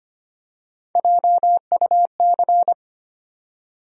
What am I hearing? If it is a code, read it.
Morse code, heard as a keyed tone, about 25 wpm, 700 Hz. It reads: JUC
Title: USA Call Sign Postfixes - 3 Letters 25wpm